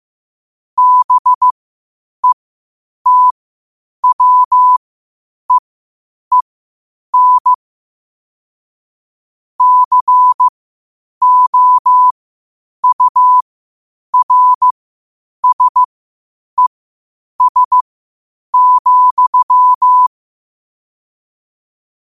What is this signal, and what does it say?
Morse code, heard as a keyed tone, about 15 wpm, spaced as for 8 wpm, 1000 Hz. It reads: BETWEEN COURSES,
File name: qotdfiles / 8wpm